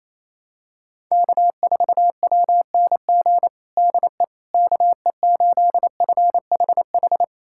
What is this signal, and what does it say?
Morse code, heard as a keyed tone, about 28 wpm, 700 Hz. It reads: K4WNG DE KE8FHH
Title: USA Callsign DE Callsign 28wpm